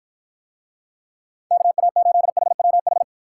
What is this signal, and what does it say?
Morse code, heard as a keyed tone, about 53 wpm, 700 Hz. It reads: KA8HWH